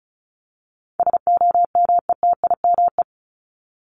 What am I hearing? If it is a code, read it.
Morse code, heard as a keyed tone, about 34 wpm, 700 Hz. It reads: SOMETIME
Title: ES:S Suffix - Top 100 Words with Base Words 35wpm